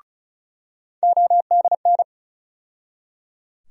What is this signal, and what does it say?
Morse code, heard as a keyed tone, about 34 wpm, 700 Hz. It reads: ODN
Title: USA Call Sign Postfixes - 3 Letters 35wpm